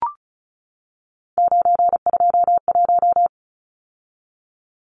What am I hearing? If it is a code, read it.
Morse code, heard as a keyed tone, about 34 wpm, 700 Hz. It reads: 921
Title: Numbers - 3-Digit 35wpm